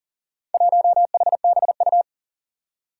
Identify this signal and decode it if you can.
Morse code, heard as a keyed tone, about 40 wpm, 700 Hz. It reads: 1HBU
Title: Four Challenging Letter-Number Combinations 40wpm